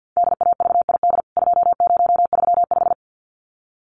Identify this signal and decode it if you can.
Morse code, heard as a keyed tone, about 50 wpm, 700 Hz. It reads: DAVID 2035
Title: CWT Exchange 50wpm